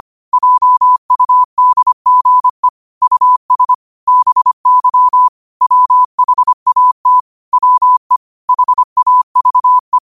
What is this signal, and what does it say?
Morse code, heard as a keyed tone, about 25 wpm, 1000 Hz. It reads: JUDGE US BY WHAT WE HAVE